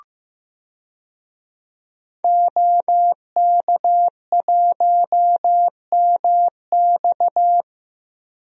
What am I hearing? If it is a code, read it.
Morse code, heard as a keyed tone, about 15 wpm, 700 Hz. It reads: OK1MX
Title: DXCC-100 Call Signs - 5 Characters - 15wpm